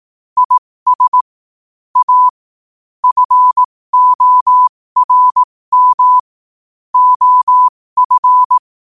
Morse code, heard as a keyed tone, about 18 wpm, spaced as for 15 wpm, 1000 Hz. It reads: IS A FORM OF
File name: qotdfiles / 15wpm